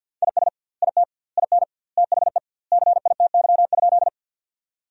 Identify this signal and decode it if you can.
Morse code, heard as a keyed tone, about 47 wpm, 700 Hz. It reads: IS IT IN THE CITY?
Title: 2x Word-Spacing - Sentences from Top 300 Words - 50wpm